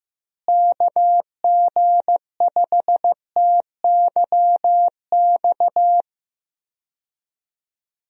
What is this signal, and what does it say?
Morse code, heard as a keyed tone, about 15 wpm, 700 Hz. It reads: KG5TYX
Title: USA Call Signs - 6 Characters - Repeated 3x - 15wpm